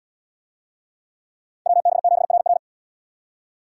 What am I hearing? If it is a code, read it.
Morse code, heard as a keyed tone, about 77 wpm, 700 Hz. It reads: WB7RR